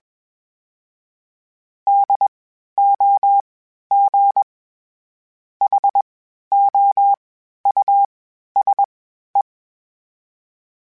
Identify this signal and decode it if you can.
Morse code, heard as a keyed tone, about 21 wpm, spaced as for 11 wpm, 800 Hz. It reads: DOGHOUSE
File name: General / General-mp3s / Houses @20